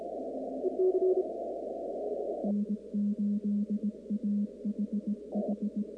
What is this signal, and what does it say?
Morse code, heard as a keyed tone, about 19 wpm, 210 Hz. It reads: N8AHH
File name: contests / 2019-cw-wpx-kt5j-k5pi- -2330 0-45